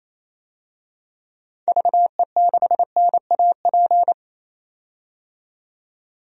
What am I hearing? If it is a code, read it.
Morse code, heard as a keyed tone, about 28 wpm, 700 Hz. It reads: VE6NAP
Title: DXCC-100 Call Signs - 6 Characters - 28wpm